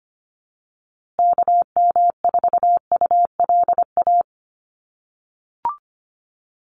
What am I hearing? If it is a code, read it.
Morse code, heard as a keyed tone, about 25 wpm, 700 Hz. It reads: KM4ULA
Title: USA Call Signs Repeated 2x 25wpm